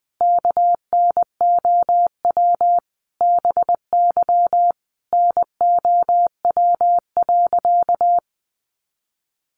Morse code, heard as a keyed tone, about 19 wpm, 700 Hz. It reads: KNOW BY NOW.
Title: Sentences from Top 1400 Words 20wpm